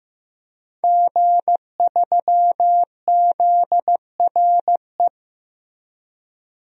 Morse code, heard as a keyed tone, about 15 wpm, 700 Hz. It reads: G3ZRE
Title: DXCC-100 Call Signs - 5 Characters - 15wpm